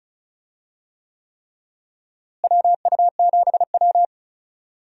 Morse code, heard as a keyed tone, about 36 wpm, 700 Hz. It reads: WU7W
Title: USA Call Signs - 4 Characters - Repeated 3x - 35wpm